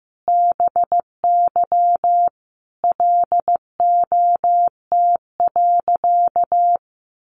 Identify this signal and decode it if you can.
Morse code, heard as a keyed tone, about 15 wpm, 700 Hz. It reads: BY LOT.
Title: Sentences from Top 1500 Words 15wpm